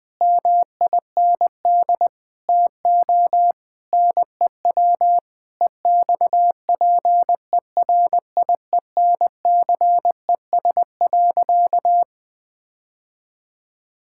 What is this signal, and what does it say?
Morse code, heard as a keyed tone, about 20 wpm, 700 Hz. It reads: MIND TO NEW EXPERIENCES.
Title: Difficult Sentences with Hints - 20wpm